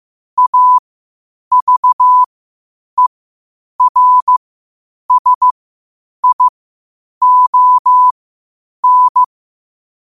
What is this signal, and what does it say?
Morse code, heard as a keyed tone, about 15 wpm, spaced as for 8 wpm, 1000 Hz. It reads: AVERSION
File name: qotdfiles / 8wpm